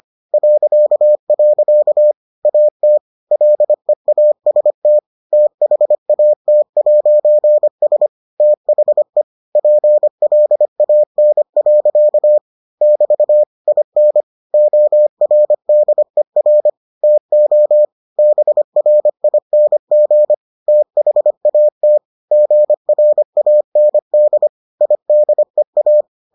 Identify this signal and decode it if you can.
Morse code, heard as a keyed tone, about 25 wpm, 600 Hz. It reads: .. AT LEAST THAT'S THE PLAN. <BT> IN ORDER TO BRING THAT GRAND IDEA